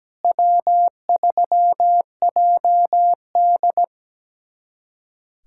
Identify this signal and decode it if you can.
Morse code, heard as a keyed tone, about 17 wpm, 700 Hz. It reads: W3JD
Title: USA Callsign DE Callsign - 4 Characters - 17wpm